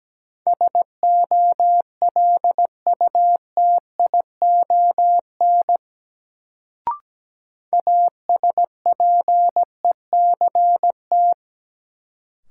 Morse code, heard as a keyed tone, about 17 wpm, 700 Hz. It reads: SOLUTION ASPECT
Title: Sentences from Top 1300 Words - New Word Drill Alt - 17wpm